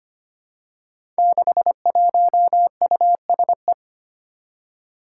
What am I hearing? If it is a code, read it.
Morse code, heard as a keyed tone, about 25 wpm, 700 Hz. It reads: 61USE
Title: Five Challenging Letter-Number Combinations 25wpm